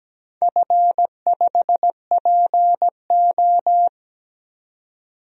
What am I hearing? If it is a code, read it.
Morse code, heard as a keyed tone, about 17 wpm, 700 Hz. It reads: F5PO
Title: DXCC-100 Call Signs - 4 Characters - 17wpm